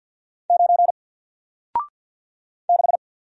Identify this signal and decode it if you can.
Morse code, heard as a keyed tone, about 50 wpm, 700 Hz. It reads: 9 6
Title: Single Letter-Number Alt 50wpm